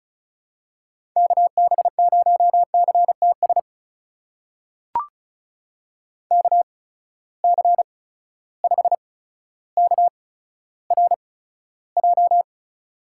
Morse code, heard as a keyed tone, about 36 wpm, spaced as for 15 wpm, 700 Hz. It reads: KB0CTS KC5KRJ
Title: USA Call Signs - 6 Characters - 8x Character Spacing - 35wpm